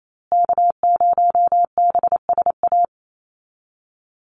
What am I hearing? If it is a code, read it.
Morse code, heard as a keyed tone, about 28 wpm, 690 Hz. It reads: K0BSA
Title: USA Call Signs Repeated 3x 28wpm